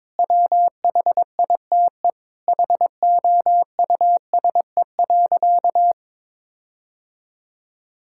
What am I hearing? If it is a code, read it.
Morse code, heard as a keyed tone, about 23 wpm, 700 Hz. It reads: WHITE HOUSE.
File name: practice / 22wpm / Sentences with 4-Word Phrases - 22wpm